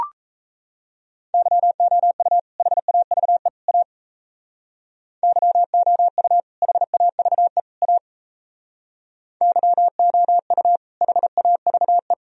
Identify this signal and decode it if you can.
Morse code, heard as a keyed tone, about 35 wpm, 700 Hz. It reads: YOU HAVE A YOU HAVE A YOU HAVE